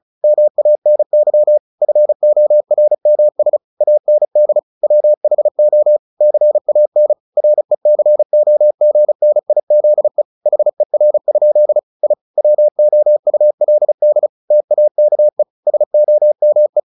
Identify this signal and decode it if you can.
Morse code, heard as a keyed tone, about 34 wpm, 600 Hz. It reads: MANY FORMS AND WHO CAN RECOGNIZE HER? I WOULD TAKE SOME
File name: the old man and the sea 2 35wpm